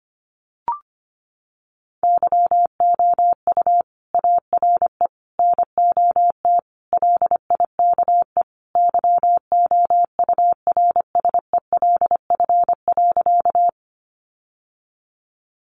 Morse code, heard as a keyed tone, about 25 wpm, 700 Hz. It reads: YOU ARE NOT LIKE YOURSELF.
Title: Sentences from Top 600 Words 25wpm